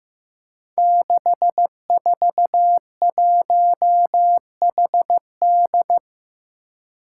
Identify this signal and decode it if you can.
Morse code, heard as a keyed tone, about 15 wpm, 700 Hz. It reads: 641HD
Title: Five Challenging Letter-Number Combinations 15wpm